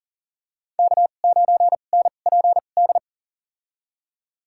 Morse code, heard as a keyed tone, about 40 wpm, spaced as for 29 wpm, 700 Hz. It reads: K9NPD
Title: USA Call Signs - 5 Characters - 2x Character Spacing - 40wpm